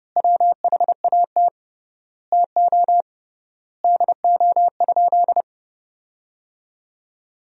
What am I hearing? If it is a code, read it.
Morse code, heard as a keyed tone, about 32 wpm, 700 Hz. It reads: WHAT TO DO?
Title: 3x Word-Spacing - Sentences from Top 400 Words - 30wpm